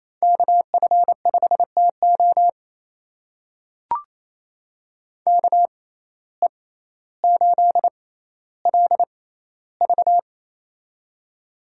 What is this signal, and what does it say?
Morse code, heard as a keyed tone, about 28 wpm, spaced as for 14 wpm, 700 Hz. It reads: KF5TO KE8LV
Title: USA Call Signs - 5 Characters - 6x Character Spacing - 28wpm